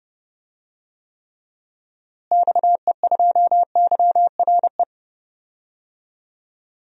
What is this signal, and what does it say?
Morse code, heard as a keyed tone, about 30 wpm, 700 Hz. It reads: XE2YRE
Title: DXCC-100 Call Signs - 6 Characters - Repeated 2x - 30wpm